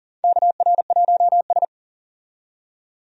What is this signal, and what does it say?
Morse code, heard as a keyed tone, about 40 wpm, 700 Hz. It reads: KR1S
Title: USA Call Signs - 4 Characters - 40wpm